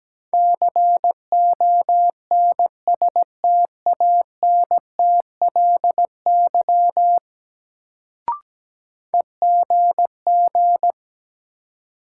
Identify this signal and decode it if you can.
Morse code, heard as a keyed tone, about 17 wpm, 700 Hz. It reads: CONSTANTLY EGG
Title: Sentences from Top 1000 Words - Review Part 7 - New Word Drill - 17wpm